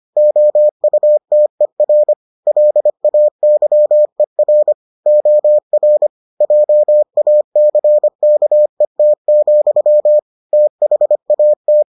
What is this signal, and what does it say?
Morse code, heard as a keyed tone, about 25 wpm, 600 Hz. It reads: OUTER LAYER OR JACKET, THAT